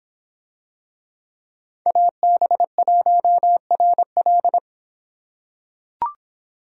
Morse code, heard as a keyed tone, about 26 wpm, 700 Hz. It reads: AB1RL